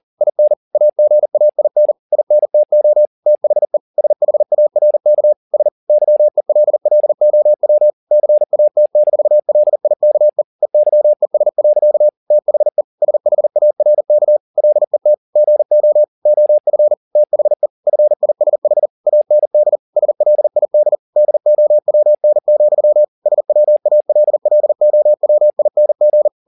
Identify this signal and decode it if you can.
Morse code, heard as a keyed tone, about 40 wpm, 600 Hz. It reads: IN AGAIN INTO THE SHARK S YELLOW CAT-LIKE EYES. THE SHARK LET GO OF THE FISH AND SLID DOWN, SWALLOWING
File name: the old man and the sea 2 40wpm